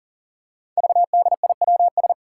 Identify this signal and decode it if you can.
Morse code, heard as a keyed tone, about 40 wpm, 700 Hz. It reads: VDIWS
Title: Five Challenging Letter-Number Combinations 40wpm